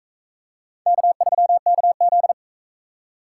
Morse code, heard as a keyed tone, about 40 wpm, 700 Hz. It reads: K3KZ